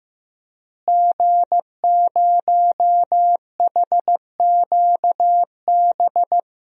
Morse code, heard as a keyed tone, about 14 wpm, 700 Hz. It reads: G0HQB
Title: DXCC-100 Call Signs - 5 Characters - Repeated 3x - 15wpm